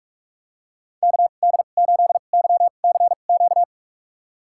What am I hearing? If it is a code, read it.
Morse code, heard as a keyed tone, about 46 wpm, spaced as for 32 wpm, 700 Hz. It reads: KD8YCQ
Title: USA Call Signs - 6 Characters - 2x Character Spacing - 45wpm